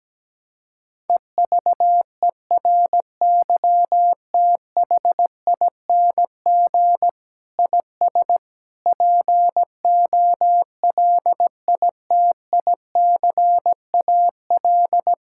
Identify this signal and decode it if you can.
Morse code, heard as a keyed tone, about 17 wpm, 700 Hz. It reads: EVERYTHING IS POLITICAL
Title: Sentences from Top 1000 Words - Review Part 5 - 17wpm